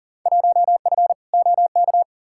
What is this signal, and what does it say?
Morse code, heard as a keyed tone, about 40 wpm, 700 Hz. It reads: 1F OK
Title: Field Day Exchanges Encore - 40wpm